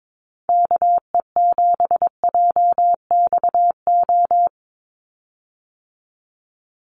Morse code, heard as a keyed tone, about 22 wpm, 700 Hz. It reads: KE7JXO
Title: USA Call Signs 22wpm